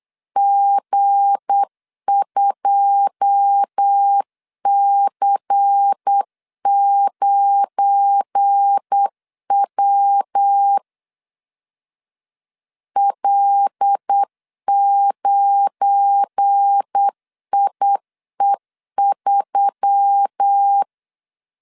Morse code, heard as a keyed tone, about 8 wpm, 800 Hz. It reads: G2C9W L9IE3
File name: LEZIONI / 9 2 8wpm 3437